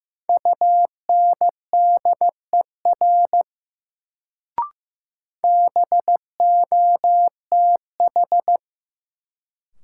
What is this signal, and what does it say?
Morse code, heard as a keyed tone, about 15 wpm, 700 Hz. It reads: UNDER BOTH
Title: Sentences from Top 300 Words - New Word Drill - 15wpm